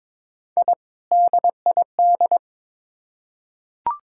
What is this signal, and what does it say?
Morse code, heard as a keyed tone, about 22 wpm, 700 Hz. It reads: I DID
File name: practice / 22wpm / Build A Phrase - 2-Word Phrases - 22wpm